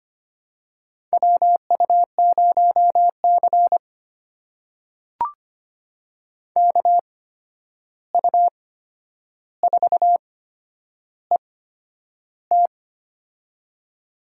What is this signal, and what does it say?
Morse code, heard as a keyed tone, about 25 wpm, spaced as for 10 wpm, 700 Hz. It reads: WU0C KU4ET
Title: USA Call Signs - 8x Character Spacing - 25wpm